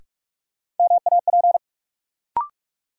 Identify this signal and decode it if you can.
Morse code, heard as a keyed tone, about 46 wpm, 700 Hz. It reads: MAP
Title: Sentences from Top 400 Words - New Word Drill - 45wpm